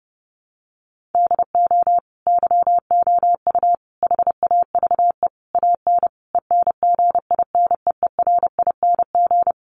Morse code, heard as a keyed tone, about 28 wpm, 700 Hz. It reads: DO YOU HAVE AN ENGINEERING
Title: Sentences from Top 1500 Words 30wpm